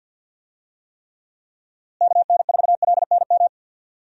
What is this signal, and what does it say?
Morse code, heard as a keyed tone, about 51 wpm, 700 Hz. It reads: KN4LNM
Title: USA Call Signs - 6 Characters - Repeated 2x - 50wpm